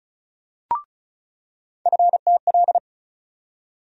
Morse code, heard as a keyed tone, about 36 wpm, 700 Hz. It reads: FTL
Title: USA Call Sign Postfixes - 3 Letters 35wpm